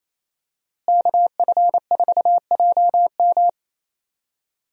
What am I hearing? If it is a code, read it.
Morse code, heard as a keyed tone, about 27 wpm, 700 Hz. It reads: KF4JM
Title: USA Call Signs - 5 Characters - Repeated 3x - 28wpm